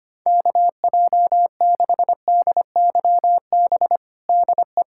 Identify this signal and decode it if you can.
Morse code, heard as a keyed tone, about 25 wpm, 700 Hz. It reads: KJ6DYB DE
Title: USA Callsign DE Callsign 25wpm